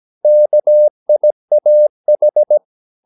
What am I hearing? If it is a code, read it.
Morse code, heard as a keyed tone, about 17 wpm, 600 Hz. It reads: KIAH
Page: Romanen Never af Ken Follet på engelsk.